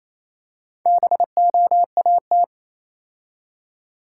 Morse code, heard as a keyed tone, about 27 wpm, 690 Hz. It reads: BOAT
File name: Easy Words - Speed-Racing with Hints - 22wpm